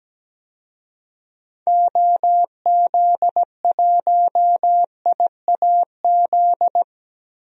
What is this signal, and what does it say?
Morse code, heard as a keyed tone, about 17 wpm, 700 Hz. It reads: OZ1IAZ
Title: DXCC-100 Call Signs - 6 Characters - Repeated 2x - 17wpm